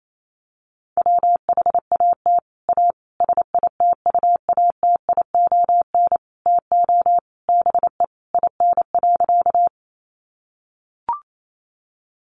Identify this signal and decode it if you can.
Morse code, heard as a keyed tone, about 28 wpm, 700 Hz. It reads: WHAT A SITUATION TO BE IN.
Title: Sentences from Top 1200 Words 28wpm